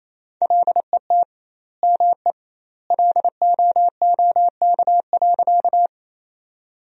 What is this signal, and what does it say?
Morse code, heard as a keyed tone, about 28 wpm, 700 Hz. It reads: LET ME LOOK.
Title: 2x Word-Spacing - Sentences from Top 300 Words - 28wpm